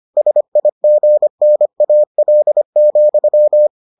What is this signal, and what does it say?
Morse code, heard as a keyed tone, about 25 wpm, 600 Hz. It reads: SIGNAL,